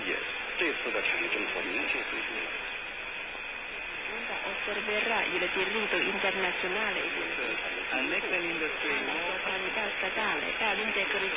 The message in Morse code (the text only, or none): none